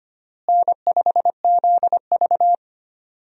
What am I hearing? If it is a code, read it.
Morse code, heard as a keyed tone, about 25 wpm, 700 Hz. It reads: N5ZV
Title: USA Call Signs - 4 Characters - Repeated 3x - 25wpm